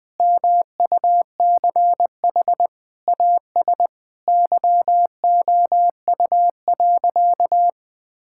Morse code, heard as a keyed tone, about 21 wpm, 700 Hz. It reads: MUCH AS YOU.
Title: Sentences with 3-Word Phrases - 20wpm